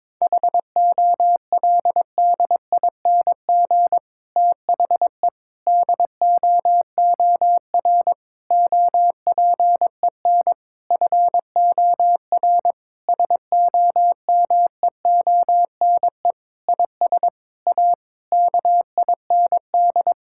Morse code, heard as a keyed tone, about 22 wpm, 700 Hz. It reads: HOLDING THE DOOR OPEN FOR SOMEONE IS A KIND
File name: Difficult Sentences with Hints - 22wpm